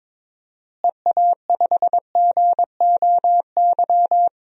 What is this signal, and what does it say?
Morse code, heard as a keyed tone, about 21 wpm, 700 Hz. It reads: EA5GOY
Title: DXCC-100 Call Signs - 6 Characters - Repeated 3x - 22wpm